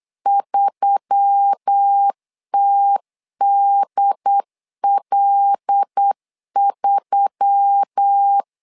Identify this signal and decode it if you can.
Morse code, heard as a keyed tone, about 8 wpm, 800 Hz. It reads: 3TDL3
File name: LEZIONI / 5 LBVC / 5 LB 8wpm 2816